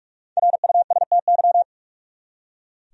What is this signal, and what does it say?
Morse code, heard as a keyed tone, about 46 wpm, 700 Hz. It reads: RUSTY
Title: USA Common Names - 45wpm